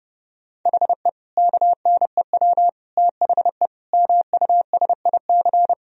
Morse code, heard as a keyed tone, about 29 wpm, 700 Hz. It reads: HE KNEW THE MUSIC
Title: Sentences from Top 1000 Words - Review Part 1 - 30wpm